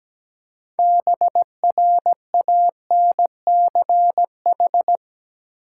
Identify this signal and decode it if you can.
Morse code, heard as a keyed tone, about 17 wpm, 700 Hz. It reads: BRANCH
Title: Six Letter Words - 17wpm